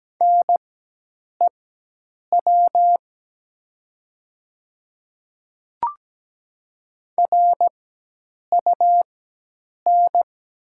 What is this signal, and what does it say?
Morse code, heard as a keyed tone, about 17 wpm, spaced as for 8 wpm, 700 Hz. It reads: NEW RUN
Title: 4x Character Spacing - Top 300 Words - 17wpm